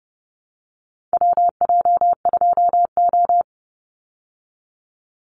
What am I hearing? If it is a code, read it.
Morse code, heard as a keyed tone, about 30 wpm, 700 Hz. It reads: WJ2O
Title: Call Signs Repeated 3x - 30wpm